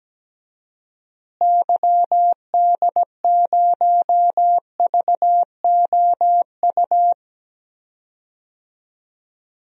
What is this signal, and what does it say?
Morse code, heard as a keyed tone, about 17 wpm, 700 Hz. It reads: YD0VOU
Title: DXCC-100 Call Signs - 6 Characters - Repeated 3x - 17wpm